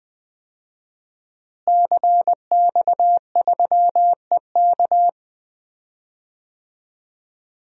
Morse code, heard as a keyed tone, about 20 wpm, 700 Hz. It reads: CX3EK
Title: DXCC-100 Call Signs - 5 Characters - Repeated 2x - 20wpm